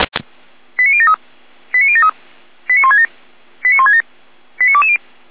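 Five-tone selective call (ZVEI), heard as quick runs of five tones, tones of about 70 ms. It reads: 89073 89073 89167 89167 89220
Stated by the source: ZVEI five-tone call 89123 received via SDR.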